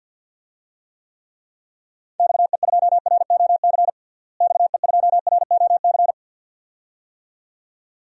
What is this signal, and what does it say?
Morse code, heard as a keyed tone, about 51 wpm, 700 Hz. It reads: XE2ROC XE2ROC
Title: DXCC-100 Call Signs - 6 Characters - Repeated 2x - 50wpm